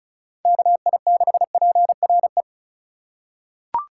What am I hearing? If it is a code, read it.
Morse code, heard as a keyed tone, about 34 wpm, 700 Hz. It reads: KI6PRE
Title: USA Call Signs - 4x Character Spacing - 35wpm